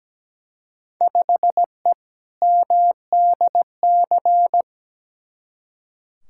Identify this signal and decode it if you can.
Morse code, heard as a keyed tone, about 17 wpm, 700 Hz. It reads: 5E MDC
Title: Field Day Exchanges - 17wpm